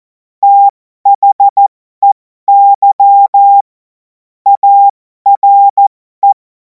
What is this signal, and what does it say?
Morse code, heard as a keyed tone, about 14 wpm, spaced as for 12 wpm, 800 Hz. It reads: THEY ARE
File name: Morsekurs / Witze